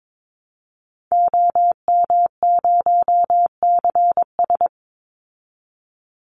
Morse code, heard as a keyed tone, about 22 wpm, 700 Hz. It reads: OM0CS
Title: Call Signs Repeated 3x - 22wpm